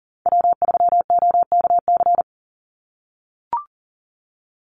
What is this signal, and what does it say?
Morse code, heard as a keyed tone, about 40 wpm, 700 Hz. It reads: W3OKC
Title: Call Signs - 40wpm